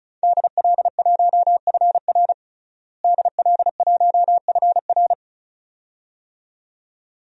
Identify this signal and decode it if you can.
Morse code, heard as a keyed tone, about 34 wpm, 700 Hz. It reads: DL1FR DL1FR
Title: DXCC-100 Call Signs - 5 Characters - Repeated 2x - 35wpm